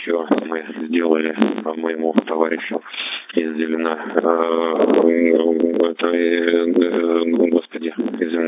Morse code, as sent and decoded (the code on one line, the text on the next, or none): none